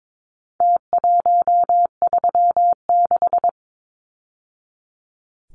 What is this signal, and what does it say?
Morse code, heard as a keyed tone, about 22 wpm, 700 Hz. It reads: T136
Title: Numbers - Contest Style - 4-Digit 22wpm